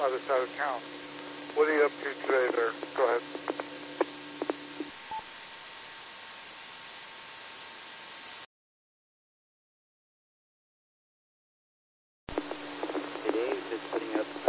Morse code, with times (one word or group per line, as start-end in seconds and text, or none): none